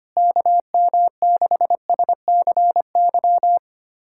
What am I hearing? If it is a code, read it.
Morse code, heard as a keyed tone, about 25 wpm, 700 Hz. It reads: KM6SCY
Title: USA Callsign DE Callsign 25wpm